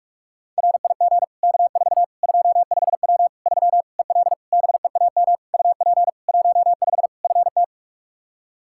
Morse code, heard as a keyed tone, about 46 wpm, 700 Hz. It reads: RIG K4 25W 3 EL BEAM UP 15 FT
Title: Practice QSO - Exchange 2 - Info - 45wpm